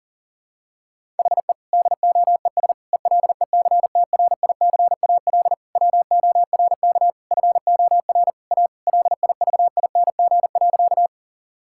Morse code, heard as a keyed tone, about 40 wpm, 700 Hz. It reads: HE DOES ELECTRICAL WORK FOR A LIVING.
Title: Sentences from Top 1000 Words - Review Part 6 - 40wpm